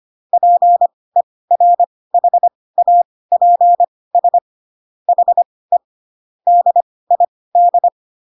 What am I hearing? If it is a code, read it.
Morse code, heard as a keyed tone, about 25 wpm, spaced as for 18 wpm, 700 Hz. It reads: PERHAPS HE DID
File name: Pinocchio / Pinocchio ch4 12 25 ik1hge